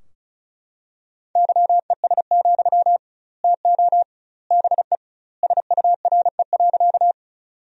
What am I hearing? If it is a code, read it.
Morse code, heard as a keyed tone, about 34 wpm, 700 Hz. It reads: YES, TO BE SURE.
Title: 2x Word-Spacing - Sentences from Top 500 Words - 35wpm